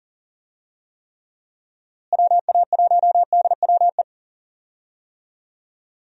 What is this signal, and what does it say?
Morse code, heard as a keyed tone, about 40 wpm, 700 Hz. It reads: WA1DWE
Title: USA Call Signs - 6 Characters - Repeated 3x - 40wpm